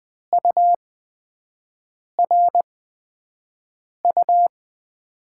Morse code, heard as a keyed tone, about 20 wpm, spaced as for 5 wpm, 700 Hz. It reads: URU
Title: USA Call Signs - 8x Character Spacing - 20wpm